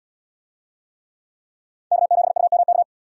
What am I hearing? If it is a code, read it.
Morse code, heard as a keyed tone, about 75 wpm, 700 Hz. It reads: K7UMK